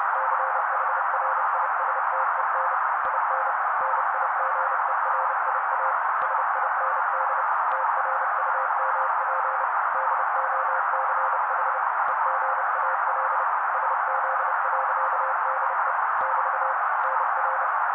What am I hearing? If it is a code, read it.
Morse code, heard as a keyed tone, about 30 wpm, 530 Hz. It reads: CHRISTEN IN NIGERIA SIND TRUMP DOCH EGAL SZ.DE <BT> NR
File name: daily / 29wpm noise